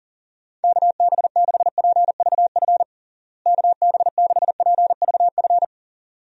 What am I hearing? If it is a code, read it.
Morse code, heard as a keyed tone, about 40 wpm, 700 Hz. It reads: KB6PVF KB6PVF
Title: USA Call Signs - 6 Characters - Repeated 2x - 40wpm